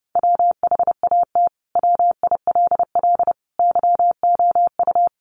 Morse code, heard as a keyed tone, about 30 wpm, 700 Hz. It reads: WHAT WILL YOU
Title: Sentences from Top 1100 Words 30wpm